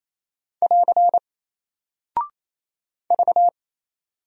Morse code, heard as a keyed tone, about 28 wpm, 700 Hz. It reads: <AR> V
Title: Single Character (Letter:Number:Punctuation:Prosign) Alt 28wpm